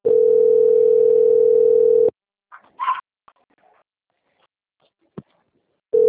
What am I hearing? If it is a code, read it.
Ringback tone (North America), 440 + 480 Hz.
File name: out- - - - - .2572